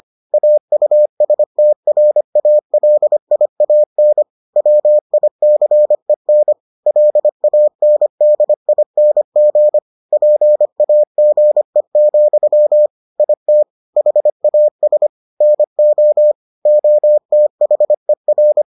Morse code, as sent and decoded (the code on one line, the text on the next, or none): .- ..- ... - .-. .- .-.. .. .- -. / .-- .. -.-. -. / .-.. .- -. -.. .. -. --. / .--. .- --. --..-- / .. - / .... .- ... / -. --- / --- - .... .-.
AUSTRALIAN WICEN LANDING PAGE, IT HAS NO OTHER